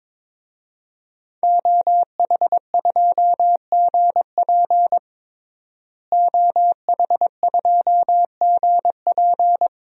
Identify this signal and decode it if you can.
Morse code, heard as a keyed tone, about 22 wpm, 700 Hz. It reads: OH2GP OH2GP
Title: DXCC-100 Call Signs - 5 Characters - Repeated 2x - 22wpm